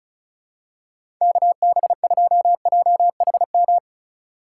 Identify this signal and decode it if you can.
Morse code, heard as a keyed tone, about 36 wpm, 700 Hz. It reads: KB2JHM
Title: USA Call Signs - 6 Characters - 2x Character Spacing - 35wpm